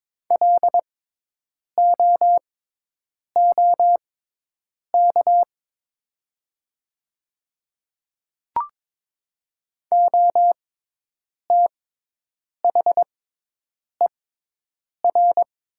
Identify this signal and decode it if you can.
Morse code, heard as a keyed tone, about 22 wpm, spaced as for 8 wpm, 700 Hz. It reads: LOOK OTHER
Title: 6x Character Spacing - Top 300 Words - 22wpm